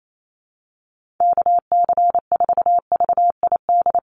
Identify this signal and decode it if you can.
Morse code, heard as a keyed tone, about 28 wpm, 700 Hz. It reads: KC4VID
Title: USA Call Signs 28wpm